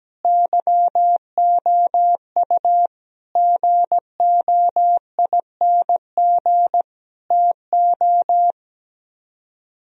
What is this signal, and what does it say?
Morse code, heard as a keyed tone, about 17 wpm, 700 Hz. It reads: YOU GOING TO
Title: Speed-Racing - Sets of 5 Words 17wpm